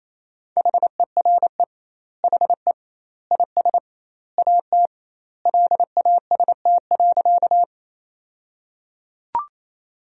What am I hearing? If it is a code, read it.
Morse code, heard as a keyed tone, about 28 wpm, 700 Hz. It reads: HERE HE IS AT LAST.
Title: 2x Word-Spacing - Sentences from Top 300 Words - 28wpm